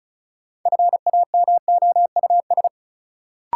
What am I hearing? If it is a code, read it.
Morse code, heard as a keyed tone, about 36 wpm, 700 Hz. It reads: FAMOUS
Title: Sentences from Top 1500 Words - New Word Drill Alt - 35wpm